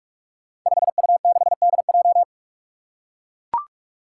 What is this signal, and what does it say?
Morse code, heard as a keyed tone, about 44 wpm, 700 Hz. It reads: 5U6DJ
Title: Five Challenging Letter-Number Combinations 45wpm